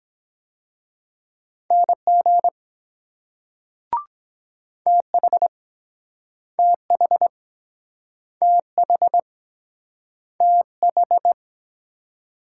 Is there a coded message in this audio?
Yes: Morse code from 1.7 to 11.3 s, heard as a keyed tone, about 23 wpm, 700 Hz. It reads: NG TH TH TH TH